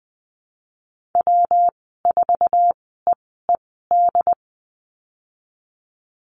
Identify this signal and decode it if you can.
Morse code, heard as a keyed tone, about 20 wpm, spaced as for 14 wpm, 700 Hz. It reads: W4EED
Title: 207 Lesson 33g - Letter j - Call Signs 20wpm